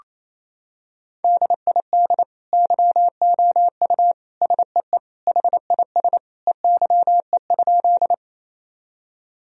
Morse code, heard as a keyed tone, about 27 wpm, 700 Hz. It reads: DID YOU SEE HIS EYE?
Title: Sentences from Top 1000 Words - Review Part 1 - 28wpm